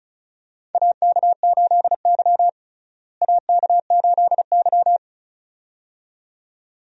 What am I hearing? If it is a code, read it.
Morse code, heard as a keyed tone, about 34 wpm, 700 Hz. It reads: AK8Y AK8Y
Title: USA Call Signs - 4 Characters - Repeated 2x - 35wpm